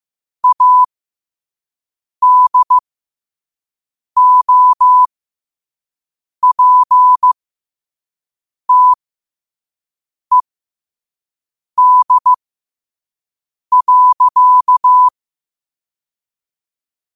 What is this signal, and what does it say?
Morse code, heard as a keyed tone, about 15 wpm, spaced as for 5 wpm, 1000 Hz. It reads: ADOPTED.